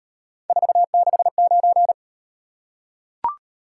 Morse code, heard as a keyed tone, about 38 wpm, 700 Hz. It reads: 469